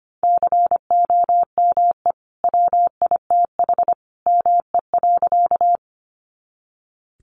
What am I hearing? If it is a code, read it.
Morse code, heard as a keyed tone, about 25 wpm, 700 Hz. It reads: COME WITH ME.
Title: Sentences from Top 200 Words 25wpm